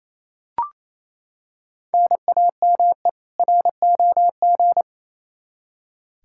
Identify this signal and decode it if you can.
Morse code, heard as a keyed tone, about 28 wpm, 700 Hz. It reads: NAME ROG
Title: Practice QSO - Exchange 1 - Beginning - 28wpm